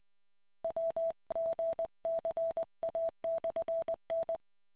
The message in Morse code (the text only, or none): WGCA/N